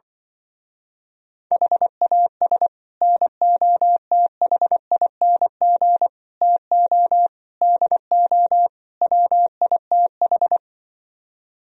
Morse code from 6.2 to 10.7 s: TO DO WITH

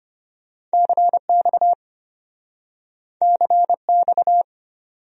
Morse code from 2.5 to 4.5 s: CX

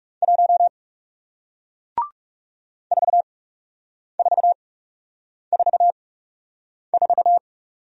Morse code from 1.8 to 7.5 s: 4444

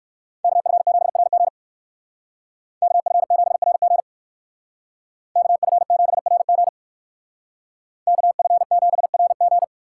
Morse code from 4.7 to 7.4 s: KF7RG

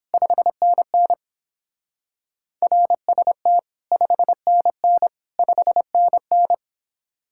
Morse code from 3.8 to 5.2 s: 5NN

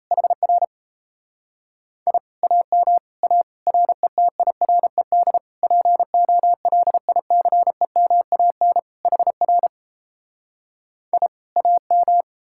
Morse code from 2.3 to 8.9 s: AM A RETIRED POLICEMAN

